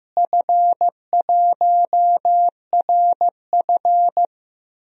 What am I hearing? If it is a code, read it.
Morse code, heard as a keyed tone, about 15 wpm, 700 Hz. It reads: F1RF
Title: DXCC-100 Call Signs - 4 Characters - Repeated 2x - 15wpm